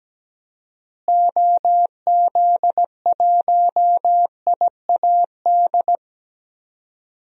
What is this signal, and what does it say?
Morse code, heard as a keyed tone, about 17 wpm, 700 Hz. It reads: OZ1IAD
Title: DXCC-100 Call Signs - 6 Characters - Repeated 2x - 17wpm